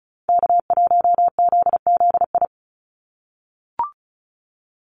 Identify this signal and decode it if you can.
Morse code, heard as a keyed tone, about 36 wpm, 700 Hz. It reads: K1ZZI
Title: Call Signs - 35wpm